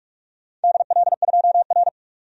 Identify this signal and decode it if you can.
Morse code, heard as a keyed tone, about 46 wpm, 700 Hz. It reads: DL2R
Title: DXCC-100 Call Signs - 4 Characters - 45wpm